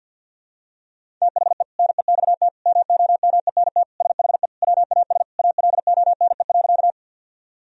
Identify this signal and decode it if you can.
Morse code, heard as a keyed tone, about 52 wpm, 700 Hz. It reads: THE NEXT MOMENT SHE WAS ALONE.
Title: Sentences from Top 1000 Words - Review Part 1 - 50wpm